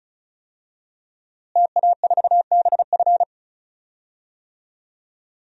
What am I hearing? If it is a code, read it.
Morse code, heard as a keyed tone, about 36 wpm, 700 Hz. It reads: TA4BF
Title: DXCC-100 Call Signs - 5 Characters - Repeated 2x - 35wpm